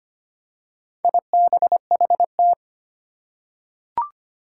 Morse code, heard as a keyed tone, about 25 wpm, 700 Hz. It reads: IBHT
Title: Four Challenging Letter-Number Combinations 25wpm